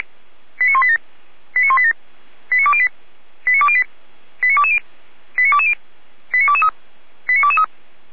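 Five-tone selective call (ZVEI), heard as quick runs of five tones, tones of about 80 ms. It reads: 89187 89187 89208 89208 89220 89220 89223 89223
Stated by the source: ZVEI five-tone selective call received on an SDR, from the feuerwehr-orlamuende (Orlamünde fire brigade) audio server.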